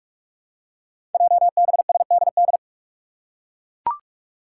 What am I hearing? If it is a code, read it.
Morse code, heard as a keyed tone, about 43 wpm, 700 Hz. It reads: JBSDD
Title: Five Challenging Letter-Number Combinations 45wpm